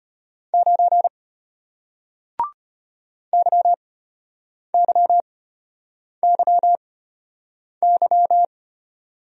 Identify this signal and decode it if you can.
Morse code, heard as a keyed tone, about 33 wpm, spaced as for 8 wpm, 700 Hz. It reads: 9 YYYY